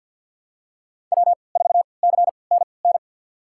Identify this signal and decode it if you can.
Morse code, heard as a keyed tone, about 46 wpm, spaced as for 27 wpm, 700 Hz. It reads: W4CNN
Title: USA Call Signs - 3x Character Spacing - 50wpm